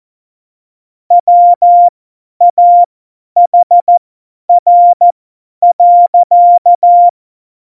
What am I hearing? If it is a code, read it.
Morse code, heard as a keyed tone, about 14 wpm, spaced as for 10 wpm, 700 Hz. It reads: WAHR.
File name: Morsekurs / Witze / Witz 24. November 2020, 50 F70